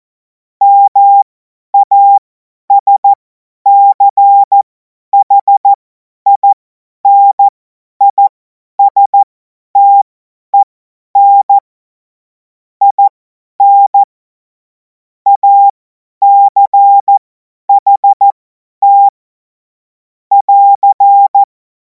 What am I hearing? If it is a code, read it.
Morse code, heard as a keyed tone, about 14 wpm, spaced as for 10 wpm, 800 Hz. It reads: MASCHINISTEN IN ACHT <AR>
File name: Morsekurs / Witze / Witz 03. März 2026 50 F70